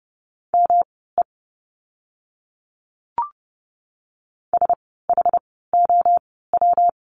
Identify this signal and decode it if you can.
Morse code, heard as a keyed tone, about 30 wpm, spaced as for 17 wpm, 700 Hz. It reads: ME SHOW
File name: course / 30:10wpm / 115 Lesson 19f - Letter w - Words 30wpm